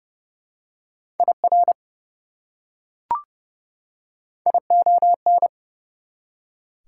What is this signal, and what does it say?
Morse code, heard as a keyed tone, about 31 wpm, 700 Hz. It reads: IR ION
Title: Prefixes and Suffixes 30wpm